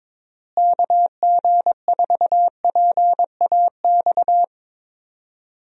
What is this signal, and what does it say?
Morse code, heard as a keyed tone, about 22 wpm, 700 Hz. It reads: KG4PAX
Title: USA Call Signs - 6 Characters - Repeated 2x - 22wpm